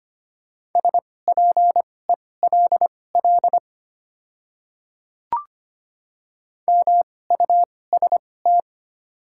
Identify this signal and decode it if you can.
Morse code, heard as a keyed tone, about 25 wpm, spaced as for 18 wpm, 690 Hz. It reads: SPELL MUST
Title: 2x Character Spacing - Top 200 Words - 25wpm